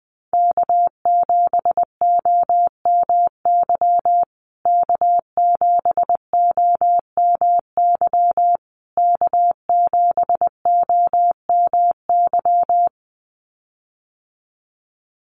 Morse code, heard as a keyed tone, about 20 wpm, 700 Hz. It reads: K7OMY K7OMY K7OMY
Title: USA Call Signs Repeated 3x 20wpm